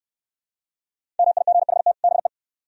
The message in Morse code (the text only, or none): MIGHT BE